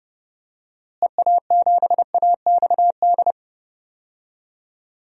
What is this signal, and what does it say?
Morse code, heard as a keyed tone, about 30 wpm, 700 Hz. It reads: EA7AXD
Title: DXCC-100 Call Signs - 6 Characters - Repeated 3x - 30wpm